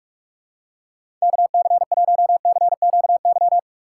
Morse code, heard as a keyed tone, about 42 wpm, 700 Hz. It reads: KC1CQY
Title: USA Call Signs - 6 Characters - 3x Character Spacing - 45wpm